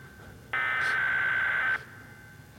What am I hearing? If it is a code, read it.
Emergency Alert System (SAME) tones (header bursts), heard as harsh buzzing bursts. It is heard from 0.5 to 1.8 s.